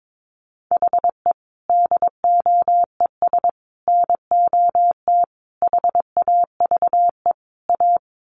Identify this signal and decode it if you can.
Morse code, heard as a keyed tone, about 22 wpm, 700 Hz. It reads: HE DOES NOT HAVE A
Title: Sets of 5 Words Encore 22wpm